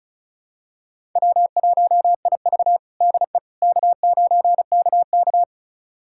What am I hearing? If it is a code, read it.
Morse code, heard as a keyed tone, about 34 wpm, 700 Hz. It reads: W1IV DE K9KK
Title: USA Callsign DE Callsign - 4 Characters - 35wpm